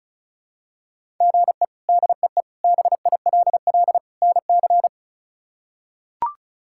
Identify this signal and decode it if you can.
Morse code, heard as a keyed tone, about 34 wpm, 700 Hz. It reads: GE DEE BILL NC
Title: K1USN SST QSO Practice - 35wpm